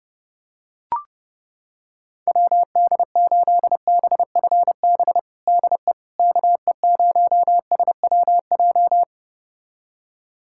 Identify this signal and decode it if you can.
Morse code, heard as a keyed tone, about 30 wpm, 700 Hz. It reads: WD8BFB DE KE0SWJ
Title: USA Callsign DE Callsign - 6 Characters - 30wpm